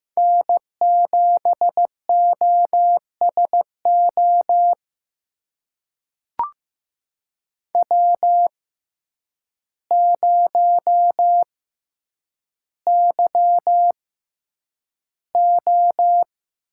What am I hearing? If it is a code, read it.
Morse code, heard as a keyed tone, about 15 wpm, spaced as for 8 wpm, 700 Hz. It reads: N7OSO W0YO
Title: USA Call Signs - 5 Characters - 6x Character Spacing - 15wpm